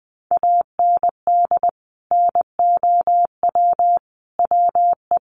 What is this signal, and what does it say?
Morse code, heard as a keyed tone, about 19 wpm, 700 Hz. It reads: AND NOW WE
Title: Sentences from Top 1500 Words 20wpm